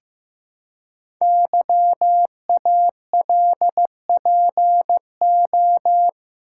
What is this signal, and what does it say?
Morse code, heard as a keyed tone, about 15 wpm, 700 Hz. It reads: YALPO
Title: Code Groups - 5 Letters - 15wpm